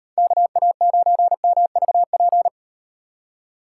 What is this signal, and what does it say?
Morse code, heard as a keyed tone, about 38 wpm, 700 Hz. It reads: KA9MVP